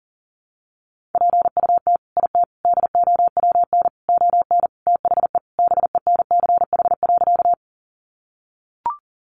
Morse code, heard as a keyed tone, about 40 wpm, 700 Hz. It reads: PUT IT DOWN ON THE BENCH.
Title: Sentences from Top 1400 Words 40wpm